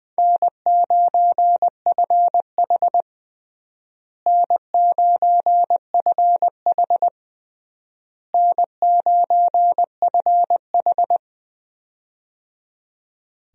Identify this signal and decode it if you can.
Morse code, heard as a keyed tone, about 20 wpm, 700 Hz. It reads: N9FH N9FH N9FH
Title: USA Call Signs - 4 Characters - Repeated 3x - 20wpm